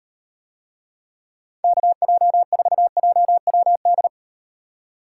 Morse code, heard as a keyed tone, about 38 wpm, 700 Hz. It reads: KJ4JWD